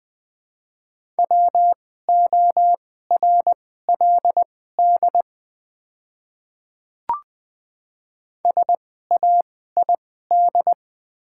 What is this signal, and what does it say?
Morse code, heard as a keyed tone, about 20 wpm, spaced as for 14 wpm, 700 Hz. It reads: WORLD SAID
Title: 2x Character Spacing - Top 300 Words - 20wpm